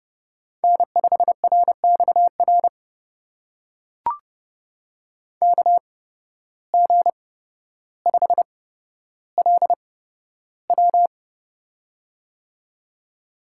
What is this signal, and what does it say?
Morse code, heard as a keyed tone, about 30 wpm, spaced as for 13 wpm, 700 Hz. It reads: N5RXR KG5LW
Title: USA Call Signs - 5 Characters - 8x Character Spacing - 30wpm